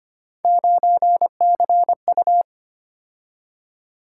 Morse code, heard as a keyed tone, about 25 wpm, 700 Hz. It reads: 9CU
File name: Three Letter-Number Combinations - 25wpm